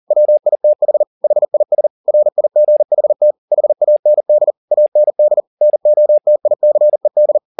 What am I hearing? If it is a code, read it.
Morse code, heard as a keyed tone, about 40 wpm, 600 Hz. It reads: WITH HIS RIGHT HAND AND NOTICED